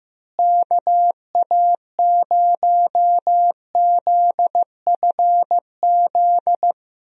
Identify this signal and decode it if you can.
Morse code, heard as a keyed tone, about 16 wpm, 700 Hz. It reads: KA0ZFZ
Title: USA Call Signs - 6 Characters - Repeated 2x - 15wpm